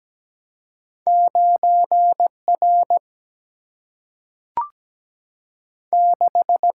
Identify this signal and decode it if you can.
Morse code, heard as a keyed tone, about 17 wpm, 700 Hz. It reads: 9 R 6
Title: Single-Letter Single-Number 17wpm